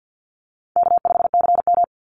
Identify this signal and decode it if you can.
Morse code, heard as a keyed tone, about 50 wpm, 700 Hz. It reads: K5CM
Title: Easy Call Signs Repeated 3x - 50wpm